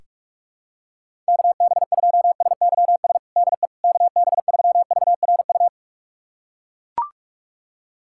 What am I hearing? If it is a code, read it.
Morse code, heard as a keyed tone, about 42 wpm, 700 Hz. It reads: KB2SYS DE KB3VRU
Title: USA Callsign DE Callsign - 6 Characters - 45wpm